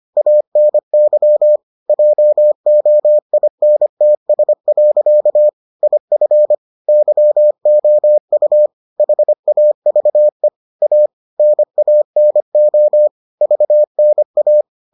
Morse code, heard as a keyed tone, about 25 wpm, 600 Hz. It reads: ANY JOINTS. IF YOU HAVE A NANO VNA